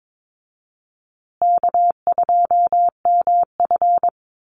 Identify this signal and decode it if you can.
Morse code, heard as a keyed tone, about 22 wpm, 700 Hz. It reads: K2MF
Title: Easy Call Signs Repeated 2x - 22wpm